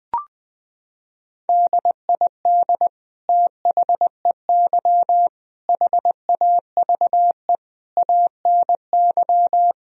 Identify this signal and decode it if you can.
Morse code, heard as a keyed tone, about 20 wpm, 700 Hz. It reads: DID THEY HAVE ANY
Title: Sentences from Top 1000 Words - Review Part 3 - 20wpm